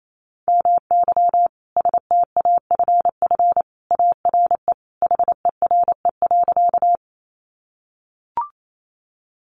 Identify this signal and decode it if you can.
Morse code, heard as a keyed tone, about 28 wpm, 700 Hz. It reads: MY STAFF ARE HERE.
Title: Sentences from Top 1200 Words 28wpm